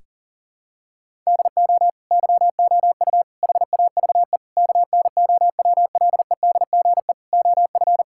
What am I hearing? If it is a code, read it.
Morse code, heard as a keyed tone, about 40 wpm, 700 Hz. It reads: DO YOU HAVE KNOWLEDGE OF
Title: Sentences from Top 1000 Words - Review Part 6 - 40wpm